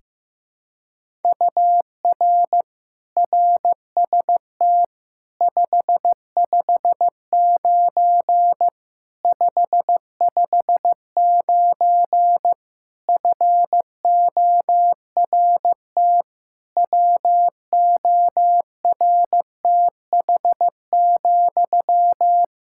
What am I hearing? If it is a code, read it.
Morse code, heard as a keyed tone, about 16 wpm, 700 Hz. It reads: UR RST 559 559 FORT WORTH,
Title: Practice QSO - Exchange 1 - Beginning - 15wpm